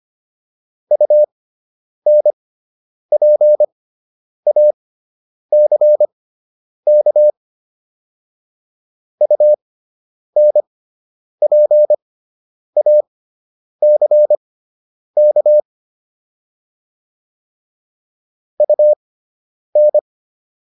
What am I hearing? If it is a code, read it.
Morse code, heard as a keyed tone, about 25 wpm, spaced as for 9 wpm, 600 Hz. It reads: UNPACK UNPACK UN